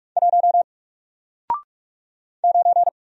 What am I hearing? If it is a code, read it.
Morse code, heard as a keyed tone, about 44 wpm, 700 Hz. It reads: 1 9
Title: Single Character (Letter:Number:Punctuation:Prosign) 45wpm